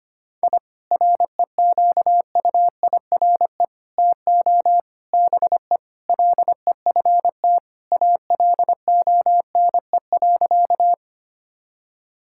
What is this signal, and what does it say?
Morse code, heard as a keyed tone, about 25 wpm, 700 Hz. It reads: I REQUIRE TO BE LEFT ALONE.
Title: Sentences from Top 1000 Words - Review Part 6 - 25wpm